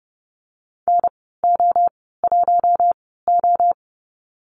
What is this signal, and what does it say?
Morse code, heard as a keyed tone, about 30 wpm, spaced as for 17 wpm, 690 Hz. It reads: NO1O
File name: Lesson 15g - Number 5 - Call Signs 30wpm